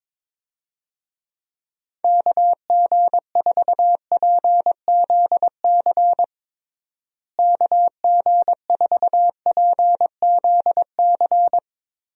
Morse code, heard as a keyed tone, about 21 wpm, 700 Hz. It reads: KG4PZC KG4PZC
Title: USA Call Signs - 6 Characters - Repeated 3x - 22wpm